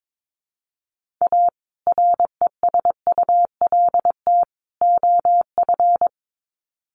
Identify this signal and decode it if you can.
Morse code, heard as a keyed tone, about 22 wpm, 700 Hz. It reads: A RESULT OF
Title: Sets of 3 Words 22wpm